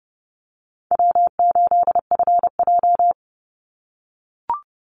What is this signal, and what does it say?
Morse code, heard as a keyed tone, about 30 wpm, 700 Hz. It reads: W8FJ
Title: Easy Call Signs - 30wpm